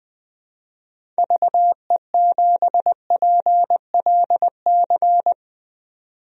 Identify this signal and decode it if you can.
Morse code, heard as a keyed tone, about 20 wpm, 700 Hz. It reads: VE7PLC
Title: DXCC-100 Call Signs - 6 Characters - 20wpm